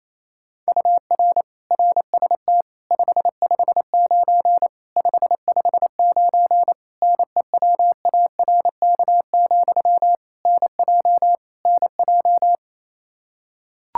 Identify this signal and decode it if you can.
Morse code, heard as a keyed tone, about 27 wpm, 700 Hz. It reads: UR RST 559 559 NEWARK, NJ NJ
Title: Practice QSO - Exchange 1 - Beginning - 28wpm